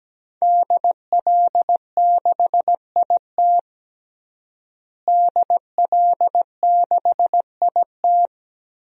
Morse code, heard as a keyed tone, about 17 wpm, 700 Hz. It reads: DL6IT DL6IT
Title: DXCC-100 Call Signs - 5 Characters - Repeated 2x - 17wpm